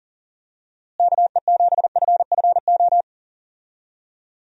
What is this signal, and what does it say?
Morse code, heard as a keyed tone, about 37 wpm, 700 Hz. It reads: KE7FFO
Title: USA Call Signs - 6 Characters - Repeated 3x - 40wpm